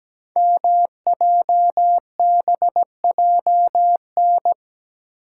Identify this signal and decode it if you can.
Morse code, heard as a keyed tone, about 17 wpm, 700 Hz. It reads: MJBJN
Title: Code Groups - 5 Letters - 17wpm